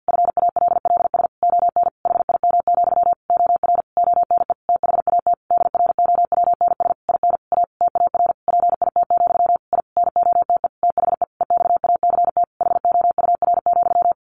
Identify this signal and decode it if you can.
Morse code, heard as a keyed tone, about 50 wpm, 700 Hz. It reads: FALLS ON HIM, OR ONE THAT DROWNS IN A TAR PIT, I NOTE THE EXACT HOUR,
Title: A Sound Of Thunder 50wpm